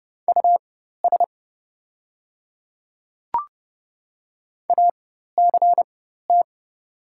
Morse code, heard as a keyed tone, about 29 wpm, spaced as for 14 wpm, 700 Hz. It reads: US ACT
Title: 4x Character Spacing - Top 300 Words - 30wpm